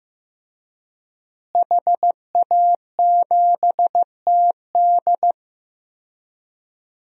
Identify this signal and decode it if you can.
Morse code, heard as a keyed tone, about 15 wpm, 700 Hz. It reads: HA7TD
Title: DXCC-100 Call Signs - 5 Characters - Repeated 3x - 15wpm